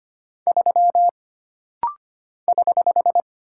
Morse code, heard as a keyed tone, about 25 wpm, 700 Hz. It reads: 3 <HH>
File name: Single Character (Letter:Number:Punctuation:Prosign) - Rapid-Fire - 25wpm